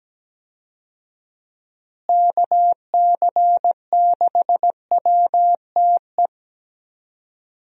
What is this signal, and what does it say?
Morse code, heard as a keyed tone, about 17 wpm, 700 Hz. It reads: KC6WTE
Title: USA Call Signs - 6 Characters - Repeated 2x - 17wpm